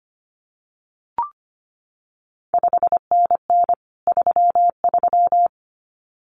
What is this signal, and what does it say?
Morse code, heard as a keyed tone, about 25 wpm, 700 Hz. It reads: 5NN 33
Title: CQ WW DX Contest Exchange - 25wpm